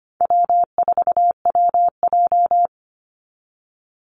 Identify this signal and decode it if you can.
Morse code, heard as a keyed tone, about 25 wpm, 700 Hz. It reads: W4WJ
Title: Easy Call Signs Repeated 3x - 25wpm